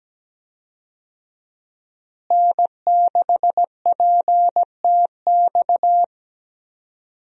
Morse code, heard as a keyed tone, about 17 wpm, 700 Hz. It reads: N6PTX
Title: USA Call Signs - 5 Characters - Repeated 3x - 17wpm